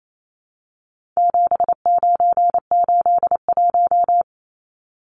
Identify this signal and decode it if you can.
Morse code, heard as a keyed tone, about 28 wpm, 700 Hz. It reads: 7981
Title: Numbers - 4-Digit 28wpm